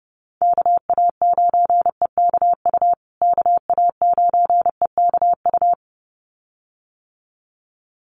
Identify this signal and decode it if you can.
Morse code, heard as a keyed tone, about 30 wpm, 700 Hz. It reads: KA9EKU KA9EKU
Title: USA Call Signs Repeated 2x 30wpm